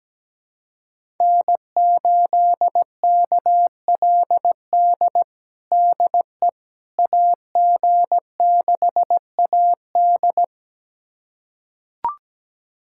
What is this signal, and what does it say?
Morse code, heard as a keyed tone, about 17 wpm, 700 Hz. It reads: N8KLD DE AG6AD
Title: USA Callsign DE Callsign - 5 Characters - 17wpm